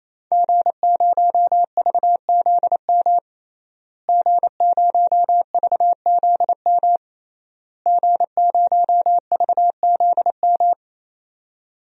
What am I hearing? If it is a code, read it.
Morse code, heard as a keyed tone, about 28 wpm, 700 Hz. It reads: G0VZM G0VZM G0VZM
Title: DXCC-100 Call Signs - 5 Characters - Repeated 3x - 28wpm